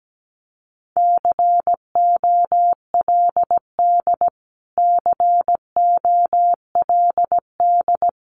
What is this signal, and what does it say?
Morse code, heard as a keyed tone, about 17 wpm, 700 Hz. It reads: COLD COLD
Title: Top 500 Words Repeated 17wpm